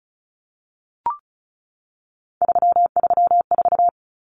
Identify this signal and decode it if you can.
Morse code, heard as a keyed tone, about 34 wpm, 700 Hz. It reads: 334
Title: Numbers - 3-Digit 35wpm